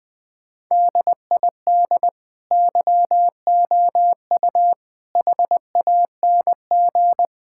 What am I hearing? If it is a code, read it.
Morse code, heard as a keyed tone, about 20 wpm, 700 Hz. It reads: DID YOU HANG
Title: Sentences from Top 1600 Words 20wpm